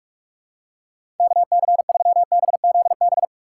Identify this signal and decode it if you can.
Morse code, heard as a keyed tone, about 46 wpm, 700 Hz. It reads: KC3BZB
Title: USA Call Signs - 6x Character Spacing - 45wpm